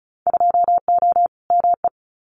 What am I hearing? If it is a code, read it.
Morse code, heard as a keyed tone, about 36 wpm, 700 Hz. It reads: 2O ME
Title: Winter Field Day Exchanges Encore 35wpm